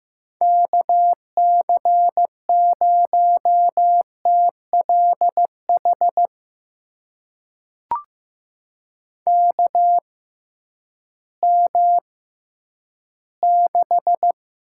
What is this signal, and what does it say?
Morse code, heard as a keyed tone, about 15 wpm, spaced as for 10 wpm, 700 Hz. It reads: KC0TLH KM6
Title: USA Call Signs - 6 Characters - 6x Character Spacing - 15wpm